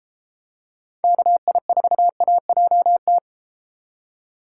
Morse code, heard as a keyed tone, about 33 wpm, 700 Hz. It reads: KI4AJT